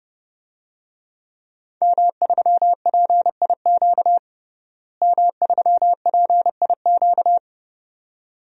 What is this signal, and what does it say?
Morse code, heard as a keyed tone, about 31 wpm, 700 Hz. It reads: M3PIQ M3PIQ
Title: DXCC-100 Call Signs - 5 Characters - Repeated 2x - 30wpm